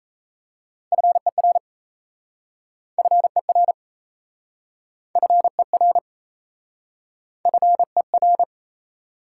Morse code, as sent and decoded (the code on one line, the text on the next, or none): ..-. .-. / ..-. .-. / ..-. .-. / ..-. .-.
FER FER FER FER